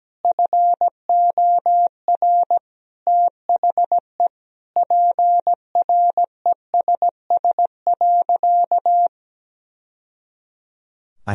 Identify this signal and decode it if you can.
Morse code, heard as a keyed tone, about 17 wpm, 700 Hz. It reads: FOR THE PRESS.
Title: Sentences from Top 1000 Words - Review Part 1 - 17wpm